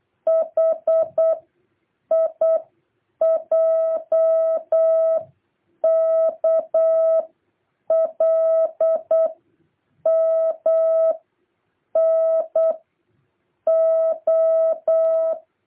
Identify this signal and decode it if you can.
Morse code, heard as a keyed tone, about 8 wpm, 650 Hz. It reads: H I J K L M N O
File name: jh8cbh / 2023cw